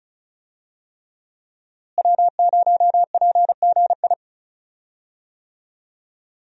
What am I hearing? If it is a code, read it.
Morse code, heard as a keyed tone, about 36 wpm, 700 Hz. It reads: W0PGI
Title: USA Call Signs - 5 Characters - Repeated 3x - 35wpm